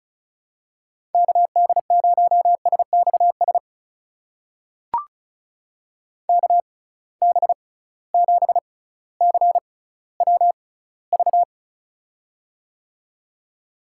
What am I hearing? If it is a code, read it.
Morse code, heard as a keyed tone, about 36 wpm, spaced as for 18 wpm, 700 Hz. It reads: KD0SXS KB7CWV
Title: USA Call Signs - 6 Characters - 6x Character Spacing - 35wpm